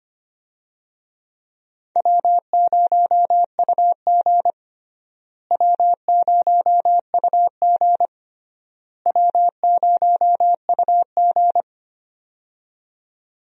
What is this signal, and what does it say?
Morse code, heard as a keyed tone, about 25 wpm, 700 Hz. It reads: W0UG W0UG W0UG
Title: USA Call Signs - 4 Characters - Repeated 3x - 25wpm